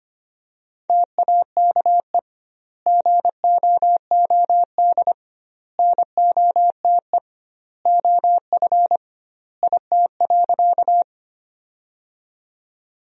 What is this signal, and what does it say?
Morse code, heard as a keyed tone, about 25 wpm, 700 Hz. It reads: TAKE GOOD NOTE OF IT.
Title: 2x Word-Spacing - Sentences from Top 500 Words - 25wpm